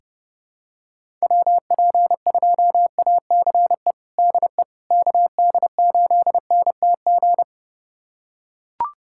Morse code, heard as a keyed tone, about 32 wpm, 700 Hz. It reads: WP2ACE DE KD8NTG
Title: USA Callsign DE Callsign - 6 Characters - 30wpm